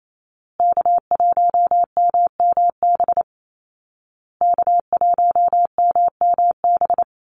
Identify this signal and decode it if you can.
Morse code, heard as a keyed tone, about 28 wpm, 700 Hz. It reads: K1MMB K1MMB
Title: Call Signs Repeated 2x - 28wpm